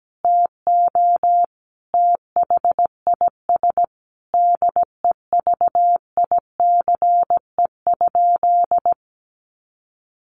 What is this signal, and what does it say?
Morse code, heard as a keyed tone, about 17 wpm, 700 Hz. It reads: TO THIS DEVICE?
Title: Sentences from Top 1500 Words 17wpm